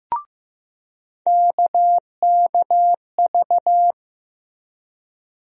Morse code, heard as a keyed tone, about 15 wpm, 700 Hz. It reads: KKV
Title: USA Call Sign Postfixes - 3 Letters 15wpm